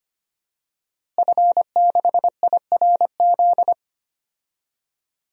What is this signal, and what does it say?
Morse code, heard as a keyed tone, about 25 wpm, 700 Hz. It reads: F6IRZ
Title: DXCC-100 Call Signs - 5 Characters - Repeated 2x - 25wpm